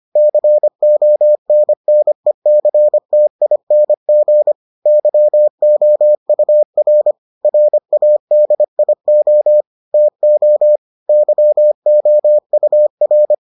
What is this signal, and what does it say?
Morse code, heard as a keyed tone, about 25 wpm, 600 Hz. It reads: CONNECTING YOUR RADIO TO YOUR